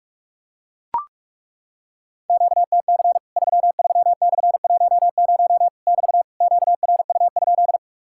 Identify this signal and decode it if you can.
Morse code, heard as a keyed tone, about 42 wpm, 700 Hz. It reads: QTC 33/10 <BT> QRU?
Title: Worked All Europe DX Contest - QTC Exchange - 45wpm